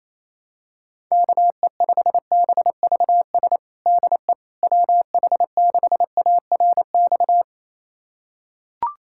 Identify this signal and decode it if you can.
Morse code, heard as a keyed tone, about 28 wpm, 700 Hz. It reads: KE5BVS DE WH6ARX
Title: USA Callsign DE Callsign - 6 Characters - 28wpm